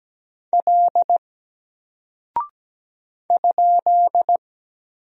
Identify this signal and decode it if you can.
Morse code, heard as a keyed tone, about 17 wpm, 700 Hz. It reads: L ?
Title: Single Character (Letter:Number:Punctuation:Prosign) 17wpm